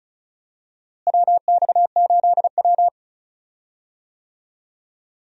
Morse code, heard as a keyed tone, about 36 wpm, 700 Hz. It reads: WX8W
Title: USA Call Signs - 4 Characters - Repeated 3x - 35wpm